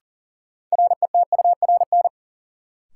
Morse code, heard as a keyed tone, about 40 wpm, 700 Hz. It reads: RETURN
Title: Sentences from Top 800 Words - New Word Drill - 40wpm